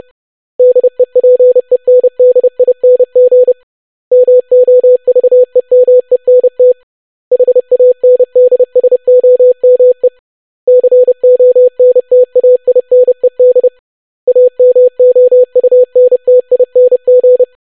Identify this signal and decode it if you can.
Morse code, heard as a keyed tone, about 30 wpm, 500 Hz. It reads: DEPENDING MOVEMENT HANDSOME CONTAINED AMOUNTING
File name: longwords-25wpm